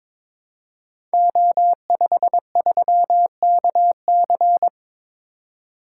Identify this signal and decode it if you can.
Morse code, heard as a keyed tone, about 22 wpm, 700 Hz. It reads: O53KC
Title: Code Groups - 5 Letter:Numbers - 22wpm